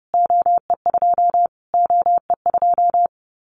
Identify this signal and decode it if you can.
Morse code, heard as a keyed tone, about 30 wpm, 700 Hz. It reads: OE2 OE2
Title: Callsign Prefixes for Top 25 DXCC - Repeated 3x - 30wpm